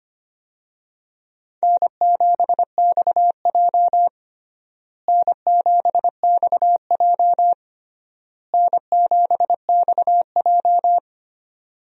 Morse code, heard as a keyed tone, about 25 wpm, 700 Hz. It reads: N7XJ N7XJ N7XJ
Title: USA Call Signs - 4 Characters - Repeated 3x - 25wpm